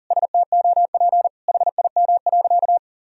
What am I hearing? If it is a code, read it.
Morse code, heard as a keyed tone, about 40 wpm, 700 Hz. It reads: STOP HIM.
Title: Sentences from Top 1000 Words - Review Part 4 - 40wpm